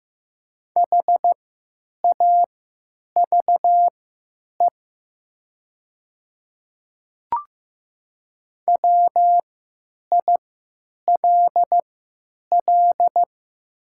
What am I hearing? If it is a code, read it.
Morse code, heard as a keyed tone, about 15 wpm, spaced as for 9 wpm, 700 Hz. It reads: HAVE WILL
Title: 3x Character Spacing - Top 100 Words - 15wpm